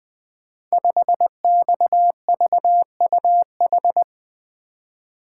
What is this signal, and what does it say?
Morse code, heard as a keyed tone, about 20 wpm, 700 Hz. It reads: 5XVUH
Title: Code Groups - 5 Letter:Numbers - 20wpm